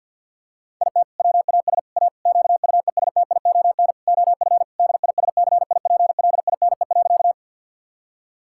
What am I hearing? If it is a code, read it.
Morse code, heard as a keyed tone, about 52 wpm, 700 Hz. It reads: IT WAS A QUESTION OF DISCIPLINE.
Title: Sentences from Top 1000 Words - Review Part 6 - 50wpm